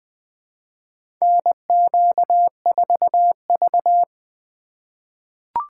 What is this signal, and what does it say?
Morse code, heard as a keyed tone, about 20 wpm, 700 Hz. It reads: NQ4V
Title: USA Call Signs - 4x Character Spacing - 20wpm